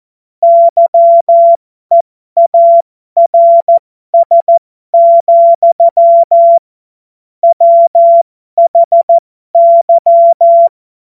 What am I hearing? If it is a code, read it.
Morse code, heard as a keyed tone, about 14 wpm, spaced as for 12 wpm, 680 Hz. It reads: YEARS, WHY